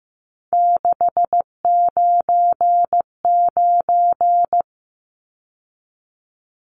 Morse code, heard as a keyed tone, about 15 wpm, 700 Hz. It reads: 699
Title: Numbers - 3-Digit 15wpm